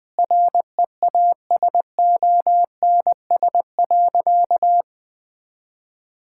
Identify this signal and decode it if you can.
Morse code, heard as a keyed tone, about 20 wpm, 700 Hz. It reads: REASONS.
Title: Sentences with 4-Word Phrases - Spoken Beginning - 20wpm